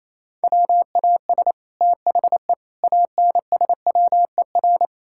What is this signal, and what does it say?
Morse code, heard as a keyed tone, about 28 wpm, 700 Hz. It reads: WAS THE ANSWER
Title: Sentences from Top 1000 Words - Review Part 6 - 28wpm